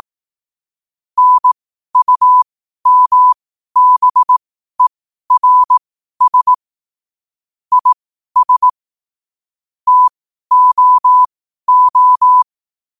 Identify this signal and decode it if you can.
Morse code, heard as a keyed tone, about 18 wpm, spaced as for 12 wpm, 1000 Hz. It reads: NUMBERS IS TOO
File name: qotdfiles / 12wpm